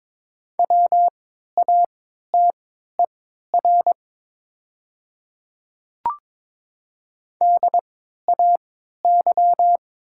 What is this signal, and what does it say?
Morse code, heard as a keyed tone, about 22 wpm, spaced as for 12 wpm, 700 Hz. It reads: WATER DAY
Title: 3x Character Spacing - Top 100 Words - 22wpm